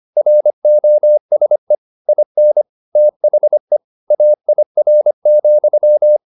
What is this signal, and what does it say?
Morse code, heard as a keyed tone, about 25 wpm, 600 Hz. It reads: ROSE IN THE AIR,
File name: the old man and the sea part 1 25wpm